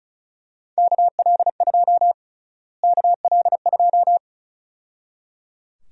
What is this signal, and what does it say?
Morse code, heard as a keyed tone, about 34 wpm, 700 Hz. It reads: KL2 KL2
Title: USA Call Sign Prefixes - Repeated 2x - 35wpm